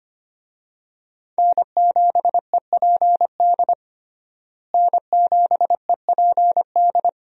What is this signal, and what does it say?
Morse code, heard as a keyed tone, about 25 wpm, 700 Hz. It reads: N7EPD N7EPD
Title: USA Call Signs - 5 Characters - Repeated 2x - 25wpm